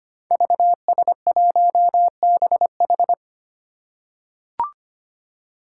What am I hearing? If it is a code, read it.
Morse code, heard as a keyed tone, about 25 wpm, 700 Hz. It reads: VS1BH
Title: Five Challenging Letter-Number Combinations 25wpm